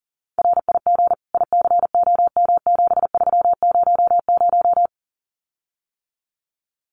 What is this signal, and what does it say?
Morse code, heard as a keyed tone, about 40 wpm, 700 Hz. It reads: RIG ICOM7300
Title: Common QSO Phrases Encore 40wpm